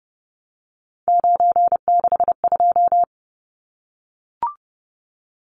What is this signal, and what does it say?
Morse code, heard as a keyed tone, about 30 wpm, 700 Hz. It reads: 962
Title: Numbers - 3-Digit 30wpm